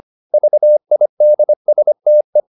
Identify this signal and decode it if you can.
Morse code, heard as a keyed tone, about 25 wpm, 600 Hz. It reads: VIDSTE
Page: Ferie 25 wpm